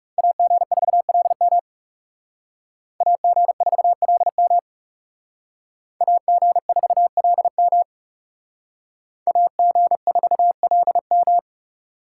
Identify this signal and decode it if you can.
Morse code, heard as a keyed tone, about 38 wpm, 700 Hz. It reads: AG4LM AG4LM AG4LM AG4LM